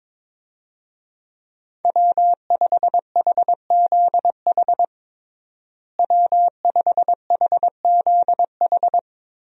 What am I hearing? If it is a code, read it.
Morse code, heard as a keyed tone, about 22 wpm, 700 Hz. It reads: W5HZH W5HZH
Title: USA Call Signs - 5 Characters - Repeated 3x - 22wpm